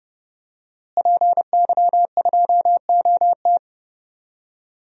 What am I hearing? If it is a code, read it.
Morse code, heard as a keyed tone, about 30 wpm, 700 Hz. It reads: PY2OT
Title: DXCC-100 Call Signs - 5 Characters - Repeated 3x - 30wpm